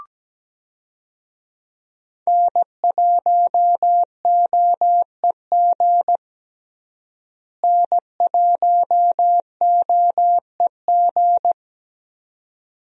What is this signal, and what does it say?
Morse code, heard as a keyed tone, about 17 wpm, 700 Hz. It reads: N1OEG N1OEG
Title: USA Call Signs - 5 Characters - Repeated 2x - 17wpm